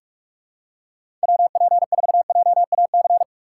Morse code, heard as a keyed tone, about 46 wpm, 700 Hz. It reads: WP4JAC
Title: USA Call Signs - 3x Character Spacing - 45wpm